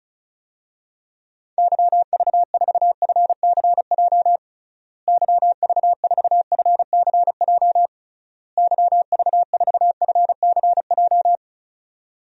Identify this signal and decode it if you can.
Morse code, heard as a keyed tone, about 36 wpm, 700 Hz. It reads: YV4FCJ YV4FCJ YV4FCJ
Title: DXCC-100 Call Signs - 6 Characters - Repeated 3x - 35wpm